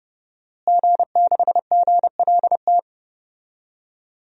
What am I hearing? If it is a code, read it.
Morse code, heard as a keyed tone, about 31 wpm, 700 Hz. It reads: G6GLT
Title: DXCC-100 Call Signs - 5 Characters - Repeated 2x - 30wpm